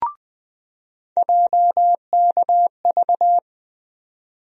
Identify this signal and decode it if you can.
Morse code, heard as a keyed tone, about 20 wpm, 700 Hz. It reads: JKV
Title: USA Call Sign Postfixes - 3 Letters 20wpm